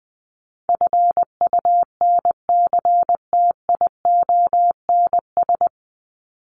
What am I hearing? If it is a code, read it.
Morse code, heard as a keyed tone, about 19 wpm, 700 Hz. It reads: FUNCTIONS
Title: ES:S Suffix - Top 100 Words with Base Words 20wpm